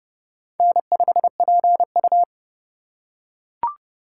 Morse code, heard as a keyed tone, about 30 wpm, 700 Hz. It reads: N5PU
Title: K1USN SST QSO Practice - 30wpm